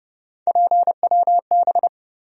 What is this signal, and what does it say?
Morse code, heard as a keyed tone, about 30 wpm, 700 Hz. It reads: PWB
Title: USA Call Sign Postfixes - 3 Letters 30wpm